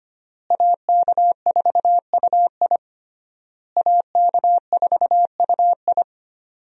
Morse code, heard as a keyed tone, about 25 wpm, 700 Hz. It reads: AK4UI AK4UI
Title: USA Call Signs - 5 Characters - Repeated 3x - 25wpm